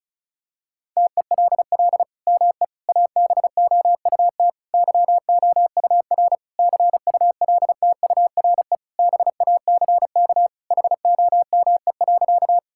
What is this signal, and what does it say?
Morse code, heard as a keyed tone, about 36 wpm, 700 Hz. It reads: TELL ME ABOUT YOUR CULTURE BACK HOME.
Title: Sentences from Top 1000 Words - Review Part 6 - 35wpm